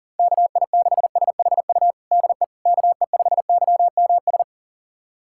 Morse code, heard as a keyed tone, about 40 wpm, 700 Hz. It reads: KI6SHU DE KE5YMS
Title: USA Callsign DE Callsign - 6 Characters - 40wpm